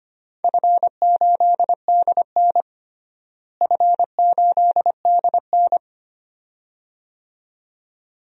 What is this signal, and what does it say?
Morse code, heard as a keyed tone, about 25 wpm, 700 Hz. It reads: F8DN F8DN
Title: DXCC-100 Call Signs - 4 Characters - Repeated 2x - 25wpm